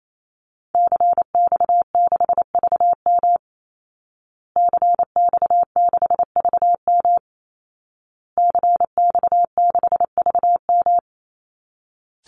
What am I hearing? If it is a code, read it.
Morse code, heard as a keyed tone, about 28 wpm, 700 Hz. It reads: CX6VM CX6VM CX6VM
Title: Call Signs Repeated 2x - 28wpm